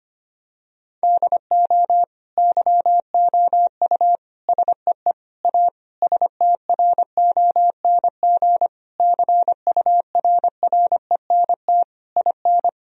Morse code, heard as a keyed tone, about 25 wpm, 700 Hz. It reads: DO YOU SEE A STRONG CURRENT IN
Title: Sentences from Top 1000 Words - Review Part 4 - 25wpm